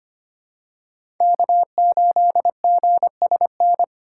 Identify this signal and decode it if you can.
Morse code, heard as a keyed tone, about 25 wpm, 700 Hz. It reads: K8GSN
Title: USA Call Signs - 5 Characters - 6x Character Spacing - 25wpm